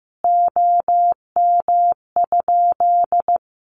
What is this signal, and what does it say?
Morse code, heard as a keyed tone, about 15 wpm, 700 Hz. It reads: OM?
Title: Common QSO Phrases 15wpm